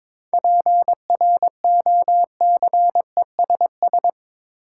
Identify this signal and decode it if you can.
Morse code, heard as a keyed tone, about 22 wpm, 700 Hz. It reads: PROCESS
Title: Sentences from Top 1000 Words - Review Part 3 - New Word Drill - 22wpm